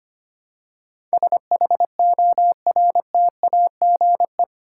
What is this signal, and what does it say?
Morse code, heard as a keyed tone, about 25 wpm, 700 Hz. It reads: SHORTAGE
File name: Eight Letter Words - 25wpm